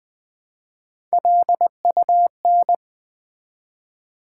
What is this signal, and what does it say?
Morse code, heard as a keyed tone, about 20 wpm, 700 Hz. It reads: LUN
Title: Build A Word - 1 Letter At A Time - 4-Letter Words - 20wpm